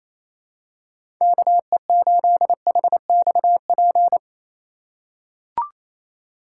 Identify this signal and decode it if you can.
Morse code, heard as a keyed tone, about 28 wpm, 700 Hz. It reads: KE8HXP
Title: USA Call Signs - 6 Characters - 4x Character Spacing - 28wpm